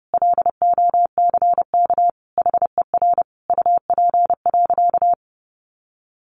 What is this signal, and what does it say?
Morse code, heard as a keyed tone, about 30 wpm, 700 Hz. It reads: LOCK HER UP.
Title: Sentences from Top 1300 Words 30wpm